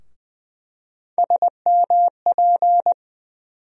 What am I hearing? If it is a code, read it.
Morse code, heard as a keyed tone, about 20 wpm, 700 Hz. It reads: SMP
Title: USA Call Sign Postfixes - 3 Letters 20wpm